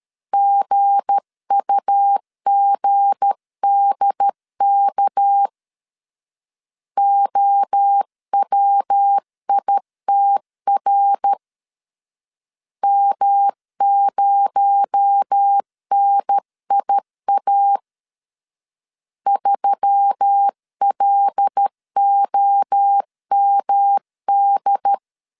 Morse code, heard as a keyed tone, about 13 wpm, 800 Hz. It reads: GUGDK OWITR M0NIA 3LOMD
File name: LEZIONI / 7 GF9Z / 7 G 12wpm 2055